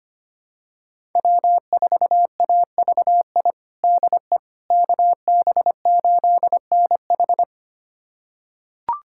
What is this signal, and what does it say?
Morse code, heard as a keyed tone, about 25 wpm, 700 Hz. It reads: W4AVI DE KB8NH
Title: USA Callsign DE Callsign - 5 Characters - 25wpm